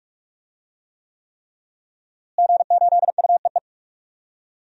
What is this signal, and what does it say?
Morse code, heard as a keyed tone, about 41 wpm, 700 Hz. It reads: G8UEE
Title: DXCC-100 Call Signs - 5 Characters - Repeated 3x - 45wpm